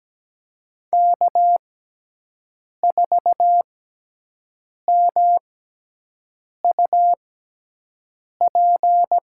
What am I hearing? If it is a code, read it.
Morse code, heard as a keyed tone, about 17 wpm, spaced as for 6 wpm, 700 Hz. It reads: K4MUP
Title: USA Call Signs - 6x Character Spacing - 17wpm